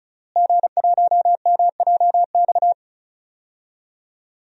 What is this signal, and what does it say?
Morse code, heard as a keyed tone, about 36 wpm, 700 Hz. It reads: G1MJX
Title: DXCC-100 Call Signs - 5 Characters - Repeated 3x - 35wpm